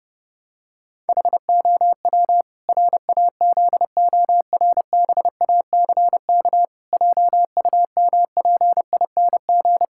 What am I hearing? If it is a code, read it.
Morse code, heard as a keyed tone, about 30 wpm, 700 Hz. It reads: HOW RAZORBACK JUMPING
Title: Pangram Challenge v2 - 30wpm